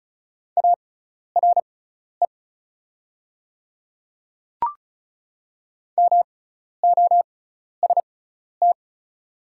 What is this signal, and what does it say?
Morse code, heard as a keyed tone, about 36 wpm, spaced as for 12 wpm, 700 Hz. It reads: ARE MOST
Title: 6x Character Spacing - Top 100 Words - 35wpm